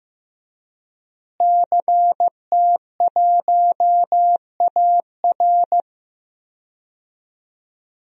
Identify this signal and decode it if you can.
Morse code, heard as a keyed tone, about 16 wpm, 700 Hz. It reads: CT1AR
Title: DXCC-100 Call Signs - 5 Characters - Repeated 3x - 15wpm